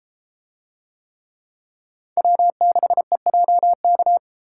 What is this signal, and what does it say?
Morse code, heard as a keyed tone, about 33 wpm, 700 Hz. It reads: W6EJK